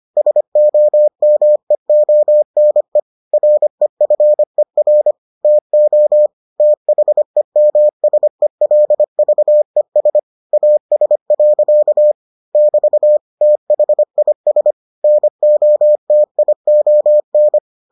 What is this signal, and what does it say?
Morse code, heard as a keyed tone, about 25 wpm, 600 Hz. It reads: SOMEONE REFER TO THEMSELVES AS. <BT> THIS NOTION